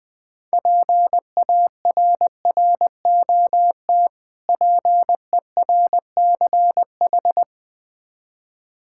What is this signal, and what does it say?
Morse code, heard as a keyed tone, about 20 wpm, 700 Hz. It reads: PARROT PERCH
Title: Difficult Words with Hints - 20wpm